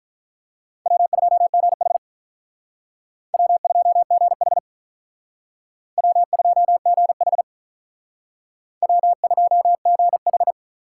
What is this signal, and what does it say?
Morse code, heard as a keyed tone, about 45 wpm, 700 Hz. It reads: W2GH W2GH W2GH W2GH